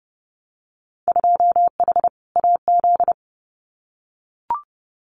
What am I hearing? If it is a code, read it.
Morse code, heard as a keyed tone, about 30 wpm, 700 Hz. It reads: 2H AZ
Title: Winter Field Day Exchanges 30wpm